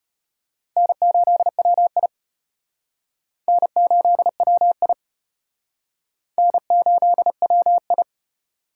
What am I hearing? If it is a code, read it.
Morse code, heard as a keyed tone, about 33 wpm, 700 Hz. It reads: N8WI N8WI N8WI